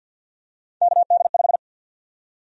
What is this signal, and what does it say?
Morse code, heard as a keyed tone, about 50 wpm, 700 Hz. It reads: KD5
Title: USA Call Sign Prefixes - 50wpm